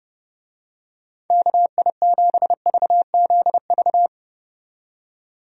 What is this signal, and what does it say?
Morse code, heard as a keyed tone, about 30 wpm, 700 Hz. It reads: KI7VZV
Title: USA Call Signs - 6 Characters - 30wpm